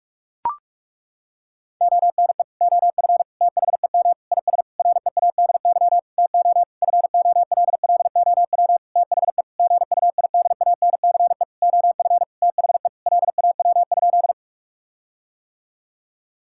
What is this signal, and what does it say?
Morse code, heard as a keyed tone, about 41 wpm, 700 Hz. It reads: ONE OF THEM IS READY TO FOLLOW THE GUIDANCE OF THE LAW?
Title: Sentences from Top 1600 Words 45wpm